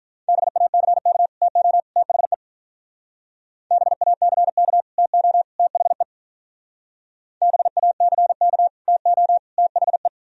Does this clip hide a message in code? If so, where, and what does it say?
Yes: Morse code from 0.3 to 10.1 s, heard as a keyed tone, about 45 wpm, 700 Hz. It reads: BACK TO THE BACK TO THE BACK TO THE